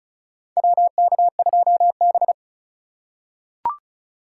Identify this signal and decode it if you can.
Morse code, heard as a keyed tone, about 36 wpm, 700 Hz. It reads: WK2B
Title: USA Call Signs - 4 Characters - 3x Character Spacing - 35wpm